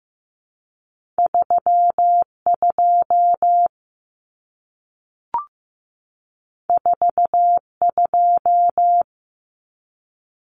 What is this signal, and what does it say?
Morse code, heard as a keyed tone, about 15 wpm, 700 Hz. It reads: 32 42
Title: Common Cut Numbers - 2-Digit 15wpm